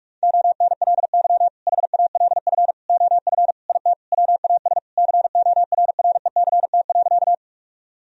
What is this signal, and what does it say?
Morse code, heard as a keyed tone, about 41 wpm, 700 Hz. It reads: ONLY HALF OF IT WAS CORRECT.
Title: Sentences from Top 1000 Words - Review Part 2 - 45wpm